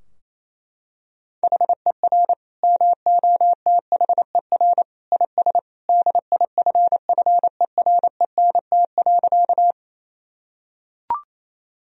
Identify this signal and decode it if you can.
Morse code, heard as a keyed tone, about 28 wpm, 700 Hz. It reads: HER MOTHER IS DIFFERENT.
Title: Sentences from Top 1000 Words - Review Part 3 - 28wpm